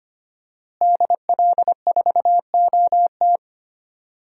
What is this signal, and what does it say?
Morse code, heard as a keyed tone, about 25 wpm, 700 Hz. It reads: DL4OT
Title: USA Call Signs - 5 Characters - Repeated 3x - 25wpm